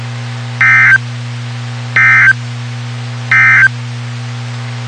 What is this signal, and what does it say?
Emergency Alert System (SAME) tones, heard as harsh buzzing bursts. It is heard from 0.5 to 3.8 s.
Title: WXK91 NOAA Weather Radio